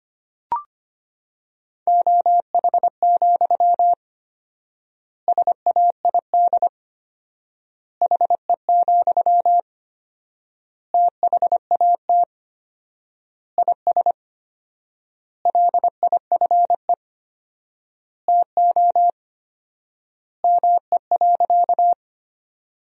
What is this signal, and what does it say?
Morse code, heard as a keyed tone, about 25 wpm, 700 Hz. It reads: OH, SAID HE, THAT IS LIFE TO ME.
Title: 4x Word-Spacing - Sentences from Top 500 Words - 25wpm